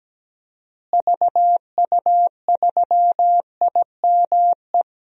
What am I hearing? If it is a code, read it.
Morse code, heard as a keyed tone, about 17 wpm, 700 Hz. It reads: VU3IME
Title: DXCC-100 Call Signs - 6 Characters - Repeated 3x - 17wpm